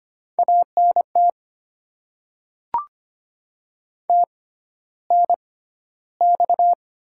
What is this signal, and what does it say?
Morse code, heard as a keyed tone, about 25 wpm, spaced as for 13 wpm, 700 Hz. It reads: ANT TNX
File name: Top 20 QSO Elements - 6x Character Spacing - 25wpm